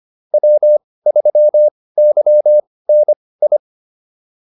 Morse code, heard as a keyed tone, about 25 wpm, spaced as for 18 wpm, 600 Hz. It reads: W3YNI
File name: POTA101 18